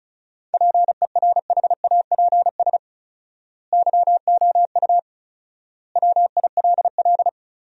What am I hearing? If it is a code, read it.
Morse code, heard as a keyed tone, about 36 wpm, 700 Hz. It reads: PERHAPS YOU WILL
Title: 4x Word-Spacing - Sentences from Top 500 Words - 35wpm